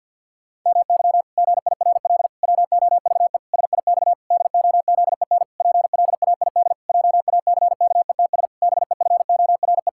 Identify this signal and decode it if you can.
Morse code, heard as a keyed tone, about 52 wpm, 700 Hz. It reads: MY GIRL WOVE SIX DOZEN PLAID JACKETS BEFORE
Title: Pangram Challenge v2 - 50wpm